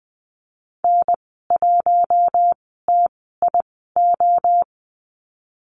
Morse code, heard as a keyed tone, about 20 wpm, spaced as for 15 wpm, 700 Hz. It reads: N1TIO
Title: 042 Lesson 8g - Number 1 - Call Signs 20wpm